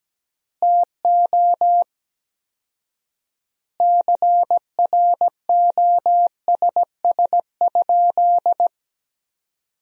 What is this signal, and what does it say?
Morse code, heard as a keyed tone, about 17 wpm, 700 Hz. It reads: TO CROSS?
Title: 4x Word-Spacing - Sentences from Top 300 Words - 17wpm